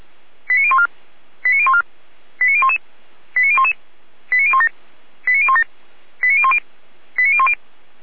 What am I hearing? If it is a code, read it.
Five-tone selective call (ZVEI), heard as quick runs of five tones, tones of about 70 ms. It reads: 89014 89014 89011 89011 89017 89017 89010 89010
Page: /455/SDR/ZVEI/89060/